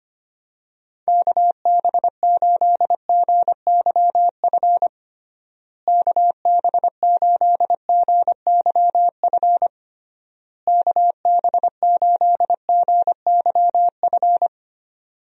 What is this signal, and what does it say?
Morse code, heard as a keyed tone, about 25 wpm, 700 Hz. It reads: KB8GYF KB8GYF KB8GYF
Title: USA Call Signs - 6 Characters - Repeated 3x - 25wpm